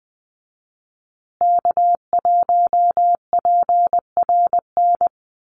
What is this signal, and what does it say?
Morse code, heard as a keyed tone, about 20 wpm, 700 Hz. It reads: K1PRN
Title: USA Call Signs Repeated 2x 20wpm